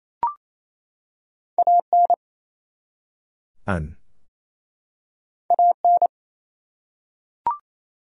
Morse code, heard as a keyed tone, about 28 wpm, 700 Hz. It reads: AN AN
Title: Top 100 Words 28wpm